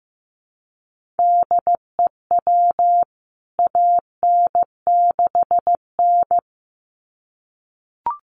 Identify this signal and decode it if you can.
Morse code, heard as a keyed tone, about 15 wpm, 700 Hz. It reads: DEW AN6N
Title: CWT Exchange 15wpm